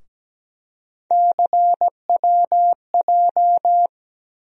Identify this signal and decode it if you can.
Morse code, heard as a keyed tone, about 17 wpm, 700 Hz. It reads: CWJ
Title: USA Call Sign Postfixes - 3 Letters 17wpm